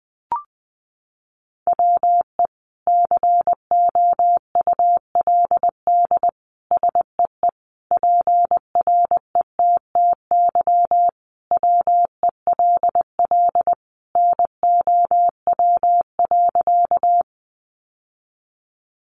Morse code, heard as a keyed tone, about 20 wpm, 700 Hz. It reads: WE COULD SEE PRETTY WELL NOW.
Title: Sentences from Top 1000 Words 20wpm